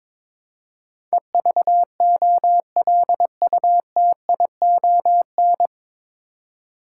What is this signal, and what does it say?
Morse code, heard as a keyed tone, about 22 wpm, 700 Hz. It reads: EVOLUTION
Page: Morse Code Ninja practice 22wpm